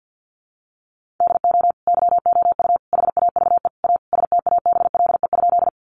Morse code, heard as a keyed tone, about 50 wpm, 700 Hz. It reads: DO YOU HAVE A STABLE?
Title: Sentences from Top 1500 Words 50wpm